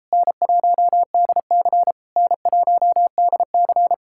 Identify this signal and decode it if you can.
Morse code, heard as a keyed tone, about 38 wpm, 700 Hz. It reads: N1DC N1DC